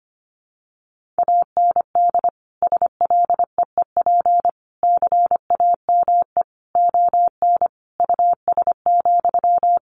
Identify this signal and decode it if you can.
Morse code, heard as a keyed tone, about 25 wpm, 700 Hz. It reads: AND SLEEP CAME ON US,
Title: Sentences from Top 1500 Words 25wpm